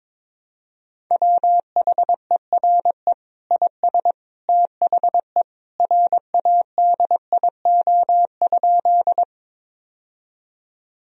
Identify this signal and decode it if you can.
Morse code, heard as a keyed tone, about 22 wpm, 700 Hz. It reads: WHERE IS THE RADIO?
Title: Sentences from Top 1000 Words - Review Part 3 - 22wpm